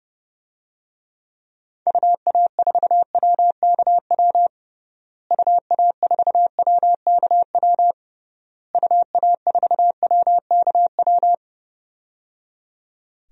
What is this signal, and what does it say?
Morse code, heard as a keyed tone, about 30 wpm, 700 Hz. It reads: UA4WKW UA4WKW UA4WKW
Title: DXCC-100 Call Signs - 6 Characters - Repeated 3x - 30wpm